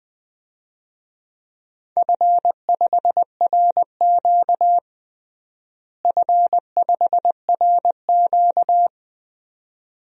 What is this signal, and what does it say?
Morse code, heard as a keyed tone, about 21 wpm, 700 Hz. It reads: F5RQ F5RQ
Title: DXCC-100 Call Signs - 4 Characters - Repeated 2x - 20wpm